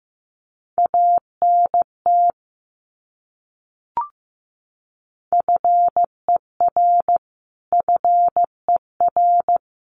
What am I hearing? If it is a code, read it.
Morse code, heard as a keyed tone, about 15 wpm, 700 Hz. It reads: ANT FER FER
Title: Abbreviations Repeated 3x - 15wpm